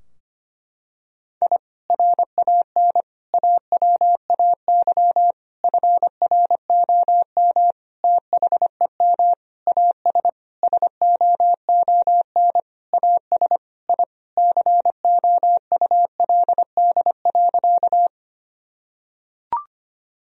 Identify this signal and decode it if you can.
Morse code, heard as a keyed tone, about 25 wpm, 700 Hz. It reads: I RAN AWAY FROM THEM AS SOON AS I COULD.
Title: Sentences from Top 1000 Words - Review Part 3 - 25wpm